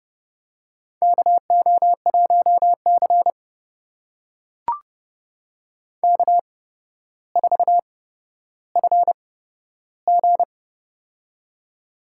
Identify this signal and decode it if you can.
Morse code, heard as a keyed tone, about 30 wpm, spaced as for 13 wpm, 700 Hz. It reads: KO1C K4FG
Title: USA Call Signs - 4 Characters - 8x Character Spacing - 30wpm